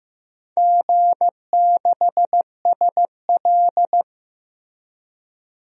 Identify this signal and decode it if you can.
Morse code, heard as a keyed tone, about 15 wpm, 700 Hz. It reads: G6SL
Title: DXCC-100 Call Signs - 4 Characters - Repeated 2x - 15wpm